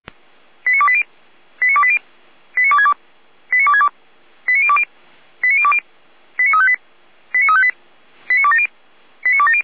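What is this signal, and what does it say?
Five-tone selective call (ZVEI), heard as quick runs of five tones, tones of about 70 ms. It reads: 89280 89280 89262 89262 89020 89020 89358 89358 89270 89270
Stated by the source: ZVEI five-tone call received by SDR, recorded by fire brigade Feuerwehr Orlamünde.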